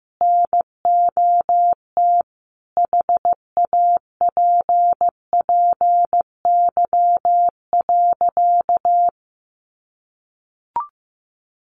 Morse code, heard as a keyed tone, about 14 wpm, 700 Hz. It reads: NOT HAPPY.
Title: Sentences from Top 1400 Words 15wpm